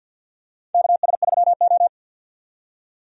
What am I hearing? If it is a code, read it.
Morse code, heard as a keyed tone, about 47 wpm, 700 Hz. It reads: KS3O
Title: USA Call Signs - 4 Characters - 50wpm